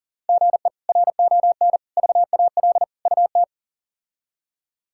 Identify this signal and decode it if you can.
Morse code, heard as a keyed tone, about 40 wpm, 700 Hz. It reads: GE RON VAL UT
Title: K1USN SST QSO Practice - 40wpm